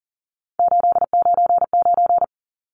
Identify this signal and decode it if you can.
Morse code, heard as a keyed tone, about 39 wpm, 700 Hz. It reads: 899
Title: Numbers - 3-Digit 40wpm